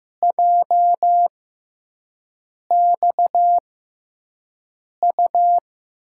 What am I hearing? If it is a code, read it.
Morse code, heard as a keyed tone, about 15 wpm, spaced as for 5 wpm, 700 Hz. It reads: JXU
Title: USA Call Signs - 5 Characters - 6x Character Spacing - 15wpm